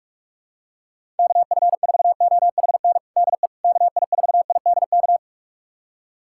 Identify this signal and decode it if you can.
Morse code, heard as a keyed tone, about 45 wpm, 700 Hz. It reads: KF4OHN DE KI4IDK
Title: USA Callsign DE Callsign - 6 Characters - 45wpm